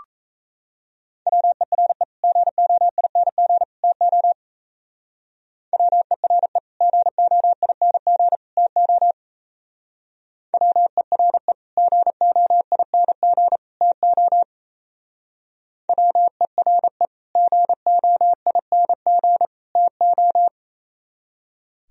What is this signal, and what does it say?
Morse code, heard as a keyed tone, about 36 wpm, 700 Hz. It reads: WERE GOING TO WERE GOING TO WERE GOING TO WERE GOING TO